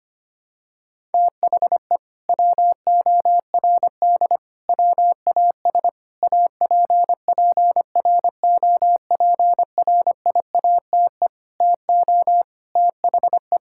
Morse code, heard as a keyed tone, about 25 wpm, 700 Hz. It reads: THE WORD WAS APPROPRIATE TO THE
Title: Sentences from Top 1000 Words - Review Part 5 - 25wpm